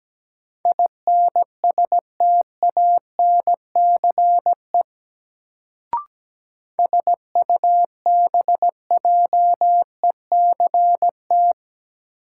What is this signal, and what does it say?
Morse code, heard as a keyed tone, about 17 wpm, 700 Hz. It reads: INSTANCE SUBJECT
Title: Sentences from Top 800 Words - New Word Drill - 17wpm